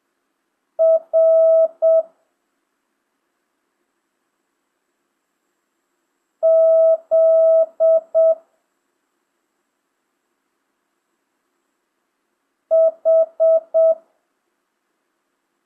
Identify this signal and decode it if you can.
Morse code, heard as a keyed tone, about 7 wpm, 650 Hz. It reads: R Z H
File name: jh8cbh / 2025cw